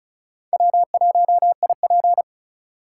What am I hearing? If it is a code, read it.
Morse code, heard as a keyed tone, about 34 wpm, 700 Hz. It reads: W1IP
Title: USA Call Signs - 4 Characters - 4x Character Spacing - 35wpm